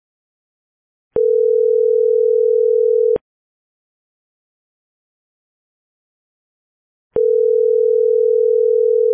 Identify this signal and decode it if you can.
Ringback tone (North America), 440 + 480 Hz.